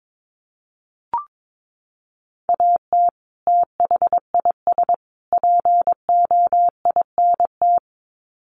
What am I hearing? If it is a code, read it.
Morse code, heard as a keyed tone, about 22 wpm, 700 Hz. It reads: AT THIS POINT
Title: Sets of 3 Words 22wpm